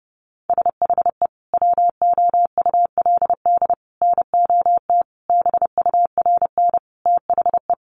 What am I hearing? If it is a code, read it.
Morse code, heard as a keyed tone, about 30 wpm, 700 Hz. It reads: SHE WOULD NOT BURN THE
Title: Sentences from Top 1300 Words 30wpm